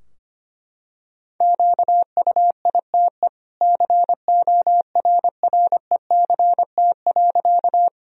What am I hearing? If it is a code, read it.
Morse code, heard as a keyed tone, about 25 wpm, 700 Hz. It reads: QUITE CORRECT.
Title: Sentences from Top 1000 Words - Review Part 2 - 25wpm